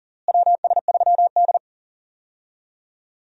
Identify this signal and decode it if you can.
Morse code, heard as a keyed tone, about 40 wpm, 700 Hz. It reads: WS3D
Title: USA Call Signs - 4 Characters - Repeated 2x - 40wpm